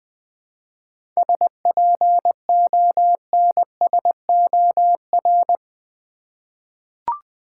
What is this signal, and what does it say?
Morse code, heard as a keyed tone, about 20 wpm, 700 Hz. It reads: SPONSOR
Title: 7 Letter Words - First 3 Letters Given - 20wpm